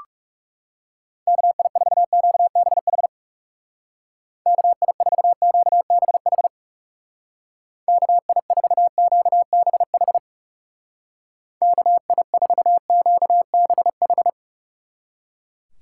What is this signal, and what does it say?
Morse code, heard as a keyed tone, about 37 wpm, 700 Hz. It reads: KI4QBH KI4QBH KI4QBH KI4QBH
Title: Speed Racing - USA 2x3 Call Signs 30wpm